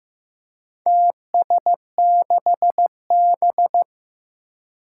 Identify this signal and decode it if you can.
Morse code, heard as a keyed tone, about 15 wpm, 700 Hz. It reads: TS6B
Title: Four Challenging Letter-Number Combinations 15wpm